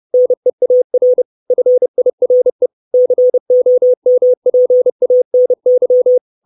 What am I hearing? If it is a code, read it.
Morse code, heard as a keyed tone, about 30 wpm, 500 Hz. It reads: NEAR FIRE COMPANY
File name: 632 Words 30WPM